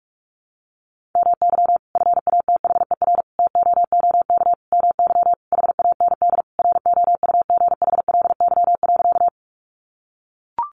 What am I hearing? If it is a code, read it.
Morse code, heard as a keyed tone, about 44 wpm, 700 Hz. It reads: MY FATHER TOOK MY HAND ROUGHLY.
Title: Sentences from Top 1500 Words 45wpm